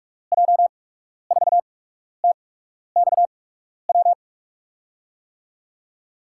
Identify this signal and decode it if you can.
Morse code, heard as a keyed tone, about 46 wpm, spaced as for 12 wpm, 700 Hz. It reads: J4TXW
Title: USA Call Signs - 6 Characters - 8x Character Spacing - 45wpm